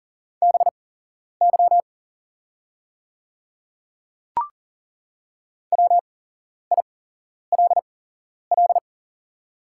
Morse code, heard as a keyed tone, about 40 wpm, spaced as for 11 wpm, 690 Hz. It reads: BY WILL
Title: 8x Character Spacing - Top 200 Words - 40wpm